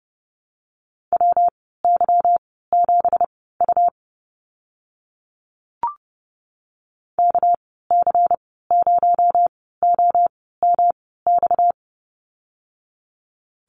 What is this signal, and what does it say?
Morse code, heard as a keyed tone, about 31 wpm, spaced as for 17 wpm, 700 Hz. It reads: WY7U KC0OMX
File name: Lesson 37g - Letter x - Call Signs 30wpm